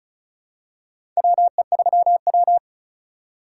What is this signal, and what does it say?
Morse code, heard as a keyed tone, about 36 wpm, 700 Hz. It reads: WE3W
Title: USA Call Signs - 4 Characters - 4x Character Spacing - 35wpm